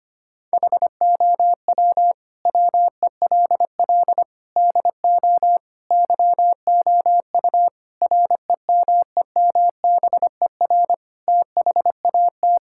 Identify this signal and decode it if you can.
Morse code, heard as a keyed tone, about 25 wpm, 700 Hz. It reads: HOW WELL DO YOU REMEMBER THAT
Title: Sentences from Top 1000 Words - Review Part 5 - 25wpm